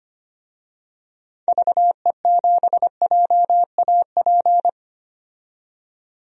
Morse code, heard as a keyed tone, about 25 wpm, 700 Hz. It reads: VE7JAP
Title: DXCC-100 Call Signs - 6 Characters - 25wpm